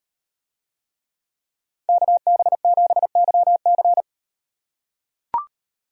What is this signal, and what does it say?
Morse code, heard as a keyed tone, about 36 wpm, 700 Hz. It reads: KB7YC